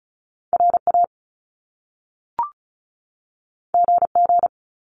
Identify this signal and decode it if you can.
Morse code, heard as a keyed tone, about 36 wpm, 700 Hz. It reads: RA GG
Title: Top Two Letter Combinations in Words 35wpm